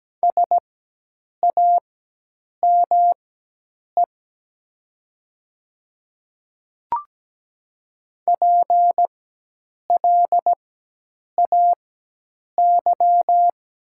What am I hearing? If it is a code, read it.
Morse code, heard as a keyed tone, about 17 wpm, spaced as for 8 wpm, 700 Hz. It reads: SAME PLAY
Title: 4x Character Spacing - Top 200 Words - 17wpm